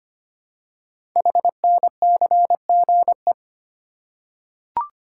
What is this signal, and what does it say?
Morse code, heard as a keyed tone, about 25 wpm, 700 Hz. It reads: HNCGE
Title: Code Groups - 5 Letters - 25wpm